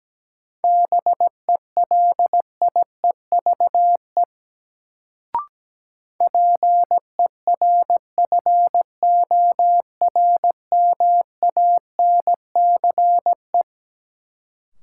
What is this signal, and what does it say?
Morse code, heard as a keyed tone, about 17 wpm, 700 Hz. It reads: BELIEVE PERFORMANCE
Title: Sentences from Top 1000 Words - New Word Drill - 17wpm